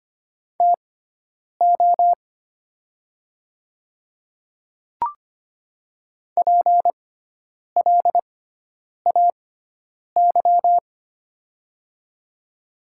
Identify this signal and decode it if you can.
Morse code, heard as a keyed tone, about 25 wpm, spaced as for 9 wpm, 690 Hz. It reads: TO PLAY
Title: 6x Character Spacing - Top 300 Words - 25wpm